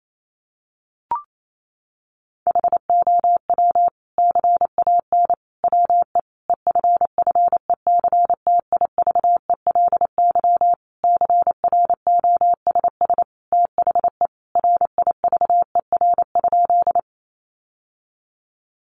Morse code, heard as a keyed tone, about 28 wpm, 700 Hz. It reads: HOW CAN WE EFFECTIVELY CROSS THE RIVER?
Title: Sentences from Top 1400 Words 28wpm